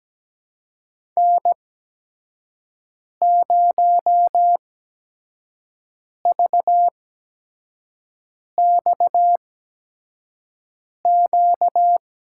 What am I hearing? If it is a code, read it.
Morse code, heard as a keyed tone, about 17 wpm, spaced as for 5 wpm, 700 Hz. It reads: N0VXQ
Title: USA Call Signs - 5 Characters - 8x Character Spacing - 17wpm